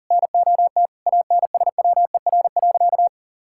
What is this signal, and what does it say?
Morse code, heard as a keyed tone, about 40 wpm, 700 Hz. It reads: NOT ANSWER.
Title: Sentences from Top 1000 Words - Review Part 6 - 40wpm